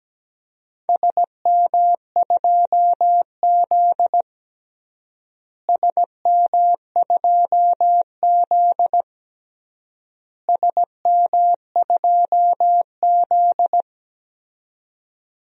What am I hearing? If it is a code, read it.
Morse code, heard as a keyed tone, about 17 wpm, 700 Hz. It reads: SM2Z SM2Z SM2Z
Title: DXCC-100 Call Signs - 4 Characters - Repeated 3x - 17wpm